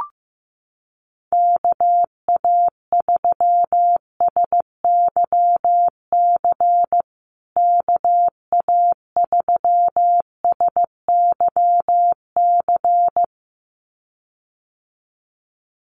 Morse code, heard as a keyed tone, about 15 wpm, 700 Hz. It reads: KA3SYC KA3SYC
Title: USA Call Signs Repeated 2x 15wpm